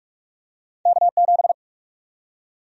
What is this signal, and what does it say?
Morse code, heard as a keyed tone, about 42 wpm, 700 Hz. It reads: K7
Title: USA Call Sign Prefixes - 45wpm